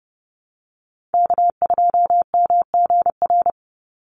Morse code, heard as a keyed tone, about 30 wpm, 700 Hz. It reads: K2MGR
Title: Call Signs Repeated 3x - 30wpm